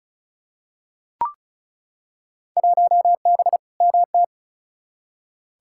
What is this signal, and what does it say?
Morse code, heard as a keyed tone, about 36 wpm, 700 Hz. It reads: 1B MT
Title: Field Day Exchanges - 35wpm